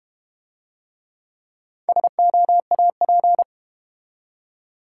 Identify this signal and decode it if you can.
Morse code, heard as a keyed tone, about 30 wpm, 700 Hz. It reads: SOAP
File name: Easy Words - Speed-Racing with Hints - 25wpm